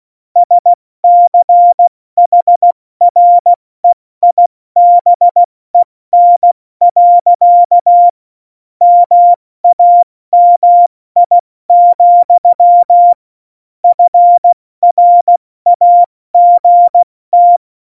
Morse code, heard as a keyed tone, about 16 wpm, 700 Hz. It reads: SCHREIBEN. MAMI, FRAGT
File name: Morsekurs / Witze / Witz 19. Februar 2019, 70 F80